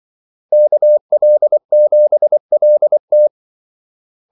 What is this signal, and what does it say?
Morse code, heard as a keyed tone, about 24 wpm, 600 Hz. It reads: KL7LT
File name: Callsigns-24wpm